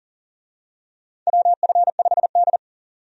Morse code, heard as a keyed tone, about 37 wpm, 700 Hz. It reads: WF5D
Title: USA Call Signs - 4 Characters - 40wpm